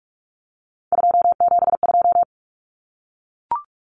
Morse code, heard as a keyed tone, about 44 wpm, 700 Hz. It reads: 272
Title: Numbers - 3-Digit 45wpm